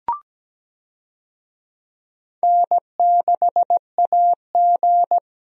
Morse code, heard as a keyed tone, about 17 wpm, 700 Hz. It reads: N6AG
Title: USA Call Signs - 4 Characters - 17wpm